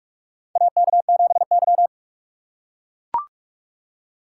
Morse code, heard as a keyed tone, about 44 wpm, 700 Hz. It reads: AK7Y
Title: USA Call Signs - 4 Characters - 3x Character Spacing - 45wpm